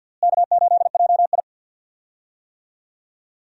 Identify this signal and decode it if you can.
Morse code, heard as a keyed tone, about 51 wpm, 700 Hz. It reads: K8JI
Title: USA Call Signs - 4 Characters - Repeated 3x - 50wpm